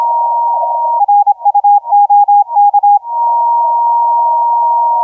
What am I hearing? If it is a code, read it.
Morse code, heard as a keyed tone, about 26 wpm, 820 Hz, with radio noise. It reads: R U OK
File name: 3A I2GPT 1429UTC 15m